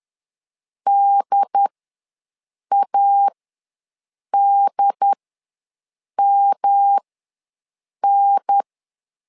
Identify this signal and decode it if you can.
Morse code, heard as a keyed tone, about 10 wpm, spaced as for 6 wpm, 800 Hz. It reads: DADMN
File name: LEZIONI / 3 MWRO / 3 MW ws 10wpm 2213